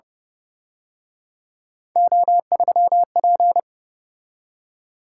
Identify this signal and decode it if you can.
Morse code, heard as a keyed tone, about 31 wpm, 700 Hz. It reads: O3P